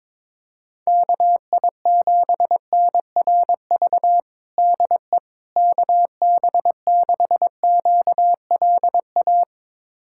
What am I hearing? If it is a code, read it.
Morse code, heard as a keyed tone, about 21 wpm, 700 Hz. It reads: KI7NRV DE KB6QLA
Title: USA Callsign DE Callsign 22wpm